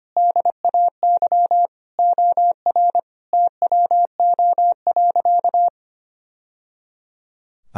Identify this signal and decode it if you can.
Morse code, heard as a keyed tone, about 25 wpm, 700 Hz. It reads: DAY OR TWO.
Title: Sentences from Top 1000 Words - Review Part 1 - 25wpm